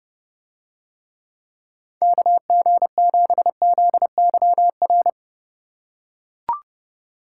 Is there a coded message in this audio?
Yes: Morse code from 2.0 to 5.1 s, heard as a keyed tone, about 29 wpm, 700 Hz. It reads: KG7ZYR